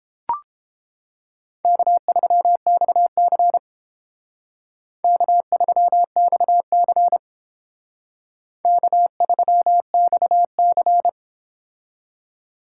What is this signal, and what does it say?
Morse code, heard as a keyed tone, about 30 wpm, 700 Hz. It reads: K3XC K3XC K3XC